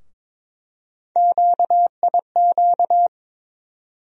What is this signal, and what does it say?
Morse code, heard as a keyed tone, about 22 wpm, 700 Hz. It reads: QIQ
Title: USA Call Sign Postfixes - 3 Letters 22wpm